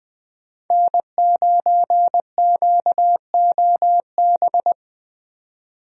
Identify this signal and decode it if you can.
Morse code, heard as a keyed tone, about 20 wpm, 700 Hz. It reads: N9QOB
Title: USA Call Signs - 5 Characters - 20wpm